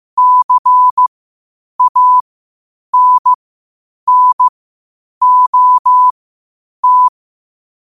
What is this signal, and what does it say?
Morse code, heard as a keyed tone, about 15 wpm, spaced as for 8 wpm, 1000 Hz. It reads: CANNOT
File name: qotdfiles / 8wpm